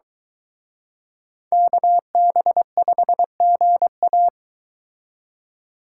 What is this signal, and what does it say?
Morse code, heard as a keyed tone, about 23 wpm, 700 Hz. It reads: KB5GA